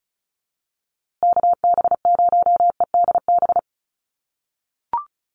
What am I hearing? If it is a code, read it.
Morse code, heard as a keyed tone, about 34 wpm, 700 Hz. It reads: KB0EDB
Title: USA Call Signs Repeated 3x 35wpm